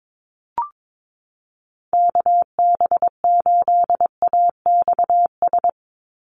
Morse code, heard as a keyed tone, about 22 wpm, 700 Hz. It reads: KB8AXS
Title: USA Call Signs 22wpm